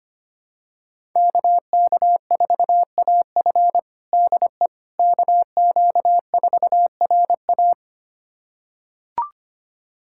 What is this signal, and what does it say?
Morse code, heard as a keyed tone, about 25 wpm, 700 Hz. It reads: KK4AF DE KQ4RA
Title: USA Callsign DE Callsign - 5 Characters - 25wpm